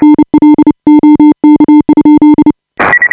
Morse code, heard as a keyed tone, about 29 wpm, 300 Hz. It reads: NL OK?
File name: AUDIO2018 / HA5BA CW